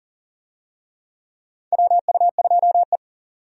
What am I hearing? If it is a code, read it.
Morse code, heard as a keyed tone, about 40 wpm, 700 Hz. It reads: WU2E
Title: USA Call Signs - 4 Characters - 40wpm